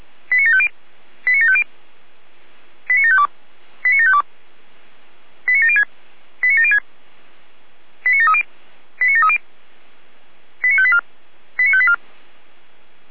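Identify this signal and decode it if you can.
Five-tone selective call (ZVEI), heard as quick runs of five tones, tones of about 70 ms. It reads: 89750 89750 89752 89752 89706 89706 89730 89730 89584 89584
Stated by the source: ZVEI five-tone selective call 89518, received on SDR.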